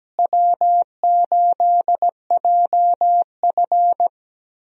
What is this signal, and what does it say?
Morse code, heard as a keyed tone, about 17 wpm, 700 Hz. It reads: W8JF
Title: USA Call Signs - 4 Characters - Repeated 2x - 17wpm